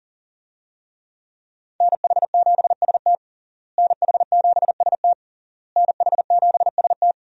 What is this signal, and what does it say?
Morse code, heard as a keyed tone, about 40 wpm, 700 Hz. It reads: NH7ST NH7ST NH7ST
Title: USA Call Signs - 5 Characters - Repeated 3x - 40wpm